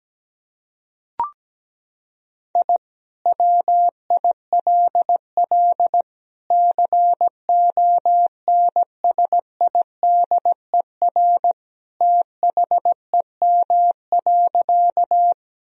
Morse code, heard as a keyed tone, about 17 wpm, 700 Hz. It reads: I WILL CONSIDER THEM.
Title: Sentences from Top 1000 Words - Review Part 4 - 17wpm